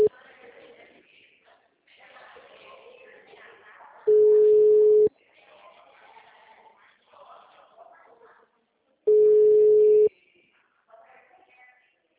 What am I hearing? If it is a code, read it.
Ringback tone (Europe, 425 Hz).